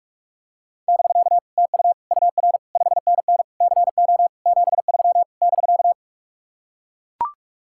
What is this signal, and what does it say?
Morse code, heard as a keyed tone, about 45 wpm, 700 Hz. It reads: <BK> TU UR 5NN CO 73 <BK>